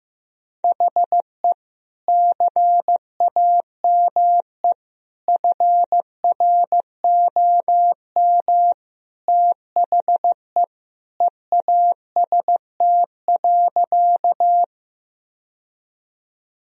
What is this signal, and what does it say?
Morse code, heard as a keyed tone, about 15 wpm, 700 Hz. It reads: HE CAME FROM THE EAST.
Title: Sentences from Top 1000 Words - Review Part 3 - 15wpm